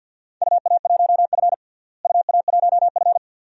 Morse code, heard as a keyed tone, about 47 wpm, 700 Hz. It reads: UA1F UA1F
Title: DXCC-100 Call Signs - 4 Characters - Repeated 2x - 50wpm